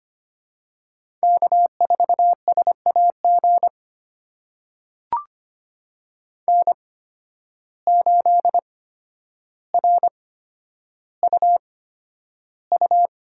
Morse code, heard as a keyed tone, about 25 wpm, spaced as for 11 wpm, 700 Hz. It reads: K4SAG N8RUU
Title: USA Call Signs - 5 Characters - 8x Character Spacing - 25wpm